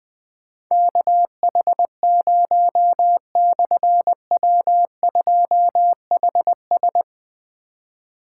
Morse code, heard as a keyed tone, about 20 wpm, 700 Hz. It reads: KH0/W2HS
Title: Difficult Call Signs 20wpm